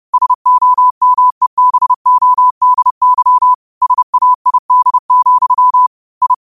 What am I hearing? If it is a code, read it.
Morse code, heard as a keyed tone, about 30 wpm, 1000 Hz. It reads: SOMEBODY SAID, I
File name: qotdfiles / 30wpm